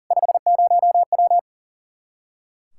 Morse code, heard as a keyed tone, about 37 wpm, 700 Hz. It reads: 50W
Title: Practice QSO - Exchange 2 - Info - 40wpm